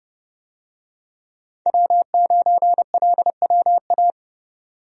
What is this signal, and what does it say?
Morse code, heard as a keyed tone, about 30 wpm, 700 Hz. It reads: W9LWA
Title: USA Call Signs - 5 Characters - 30wpm